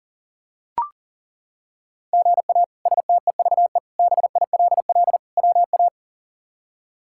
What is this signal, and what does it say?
Morse code, heard as a keyed tone, about 40 wpm, 700 Hz. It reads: GA STEVE BILL WA
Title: K1USN SST QSO Practice - 40wpm